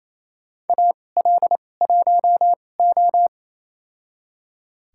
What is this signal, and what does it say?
Morse code, heard as a keyed tone, about 28 wpm, spaced as for 20 wpm, 700 Hz. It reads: AL1O
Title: USA Call Signs - 4 Characters - 2x Character Spacing - 28wpm